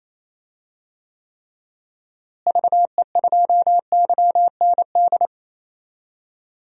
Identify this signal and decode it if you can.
Morse code, heard as a keyed tone, about 28 wpm, 700 Hz. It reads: VE2YND
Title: DXCC-100 Call Signs - 6 Characters - Repeated 3x - 28wpm